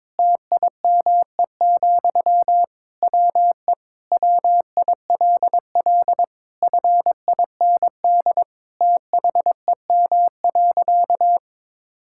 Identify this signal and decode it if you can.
Morse code, heard as a keyed tone, about 22 wpm, 700 Hz. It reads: TIME, WE WILL FIND THEM.
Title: Sentences from Top 1000 Words - Review Part 5 - 22wpm